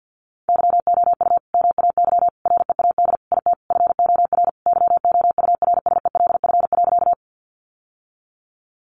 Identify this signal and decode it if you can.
Morse code, heard as a keyed tone, about 50 wpm, 700 Hz. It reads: YOU MAY READ IT FOR YOURSELF.
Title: Sentences from Top 600 Words 50wpm